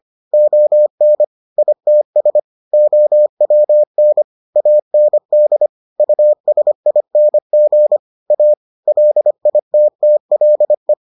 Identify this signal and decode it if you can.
Morse code, heard as a keyed tone, about 25 wpm, 600 Hz. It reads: ON ITS OWN AND USING A LITTLE